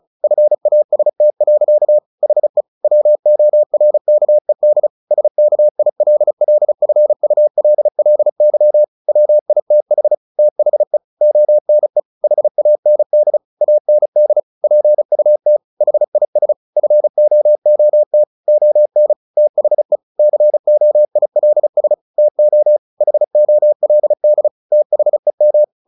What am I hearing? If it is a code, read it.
Morse code, heard as a keyed tone, about 34 wpm, 600 Hz. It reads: FAST. HE WORKED SKILLFULLY WITH THE ONE HAND AND PUT HIS FOOT ON THE COILS TO HOLD THEM
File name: the old man and the sea / the old man and the sea 1 35wpm